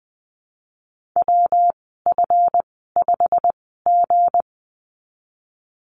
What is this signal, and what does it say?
Morse code, heard as a keyed tone, about 20 wpm, spaced as for 15 wpm, 700 Hz. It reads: WF5G
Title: 179 Lesson 29g - Slash - Call Signs 20wpm